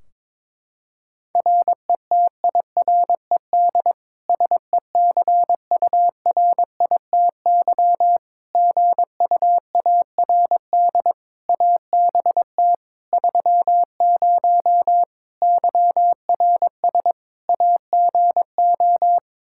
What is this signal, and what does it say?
Morse code, heard as a keyed tone, about 22 wpm, 700 Hz. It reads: RETIRED SECURITY GUARD ABT 30 YRS AGO
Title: K1USN SST QSO Practice 22wpm